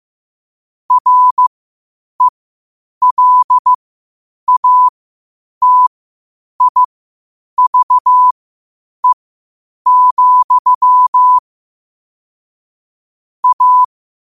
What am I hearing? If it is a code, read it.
Morse code, heard as a keyed tone, about 15 wpm, spaced as for 8 wpm, 1000 Hz. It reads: RELATIVE, A
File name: qotdfiles / 8wpm